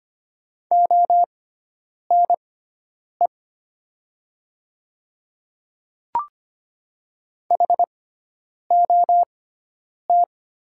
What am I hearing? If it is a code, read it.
Morse code, heard as a keyed tone, about 25 wpm, spaced as for 9 wpm, 700 Hz. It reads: ONE HOT
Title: 6x Character Spacing - Top 100 Words - 25wpm